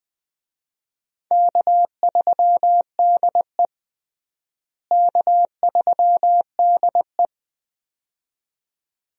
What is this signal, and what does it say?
Morse code, heard as a keyed tone, about 20 wpm, 700 Hz. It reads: K3DE K3DE
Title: USA Call Signs - 4 Characters - Repeated 2x - 20wpm